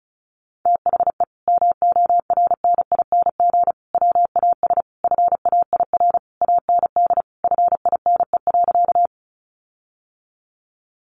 Morse code, heard as a keyed tone, about 34 wpm, 700 Hz. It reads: THE MORNING WAS FAIR AND FINE.
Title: Sentences from Top 1300 Words 35wpm